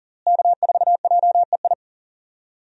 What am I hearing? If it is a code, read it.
Morse code, heard as a keyed tone, about 40 wpm, 700 Hz. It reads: K4JEI
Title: USA Call Signs - 5 Characters - Repeated 2x - 40wpm